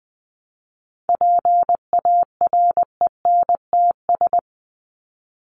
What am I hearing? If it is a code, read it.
Morse code, heard as a keyed tone, about 20 wpm, 700 Hz. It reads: PARENTS
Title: ES:S Suffix - Top 100 Words with Base Words 20wpm